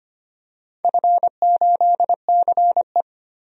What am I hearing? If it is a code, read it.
Morse code, heard as a keyed tone, about 25 wpm, 700 Hz. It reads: F8CE
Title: DXCC-100 Call Signs - 4 Characters - 25wpm